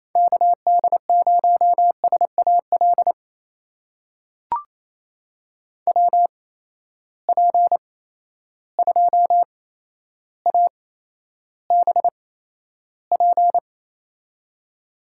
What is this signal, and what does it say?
Morse code, heard as a keyed tone, about 28 wpm, spaced as for 12 wpm, 700 Hz. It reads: KD0SAL WP2ABP
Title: USA Call Signs - 6 Characters - 8x Character Spacing - 28wpm